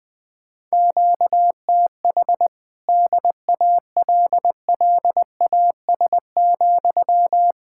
Morse code, heard as a keyed tone, about 20 wpm, 700 Hz. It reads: QTH DALLAS,
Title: Speed-Racing Common QSO Phrases Encore - 20wpm